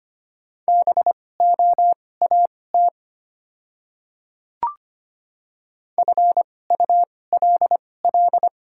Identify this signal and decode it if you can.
Morse code, heard as a keyed tone, about 25 wpm, spaced as for 18 wpm, 700 Hz. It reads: BOAT FULL
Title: 2x Character Spacing - Top 500 Words - 25wpm